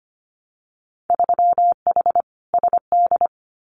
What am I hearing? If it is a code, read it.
Morse code, heard as a keyed tone, about 25 wpm, 700 Hz. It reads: 3H SD
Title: Winter Field Day Exchanges Encore 25wpm